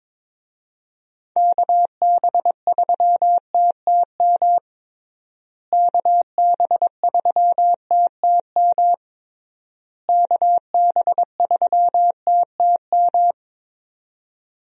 Morse code, heard as a keyed tone, about 21 wpm, 700 Hz. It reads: KB3TTM KB3TTM KB3TTM
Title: USA Call Signs - 6 Characters - Repeated 3x - 22wpm